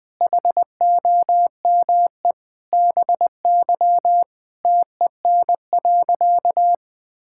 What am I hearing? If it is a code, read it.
Morse code, heard as a keyed tone, about 20 wpm, 700 Hz. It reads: HOME BY TEN.
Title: Sentences with 4-Word Phrases - Spoken Beginning - 20wpm